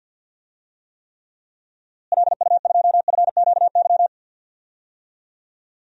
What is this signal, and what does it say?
Morse code, heard as a keyed tone, about 48 wpm, 700 Hz. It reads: LU2FQY
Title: DXCC-100 Call Signs - 6 Characters - Repeated 2x - 50wpm